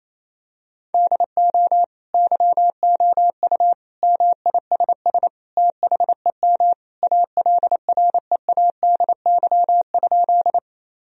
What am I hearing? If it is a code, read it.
Morse code, heard as a keyed tone, about 28 wpm, 700 Hz. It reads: DO YOU MISS THEM ALREADY?
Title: Sentences from Top 1000 Words - Review Part 3 - 28wpm